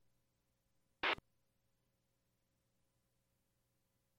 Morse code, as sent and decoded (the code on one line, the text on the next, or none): none